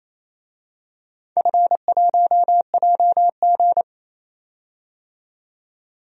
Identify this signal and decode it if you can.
Morse code, heard as a keyed tone, about 27 wpm, 700 Hz. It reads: F1JG
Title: DXCC-100 Call Signs - 4 Characters - 28wpm